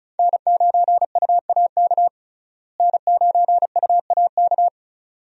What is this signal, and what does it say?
Morse code, heard as a keyed tone, about 34 wpm, 700 Hz. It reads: N9UAK N9UAK
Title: USA Call Signs - 5 Characters - Repeated 2x - 35wpm